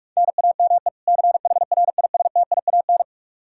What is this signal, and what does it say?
Morse code, heard as a keyed tone, about 46 wpm, 700 Hz. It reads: NAME CHRISTIAN
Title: Practice QSO - Exchange 1 - Beginning - 45wpm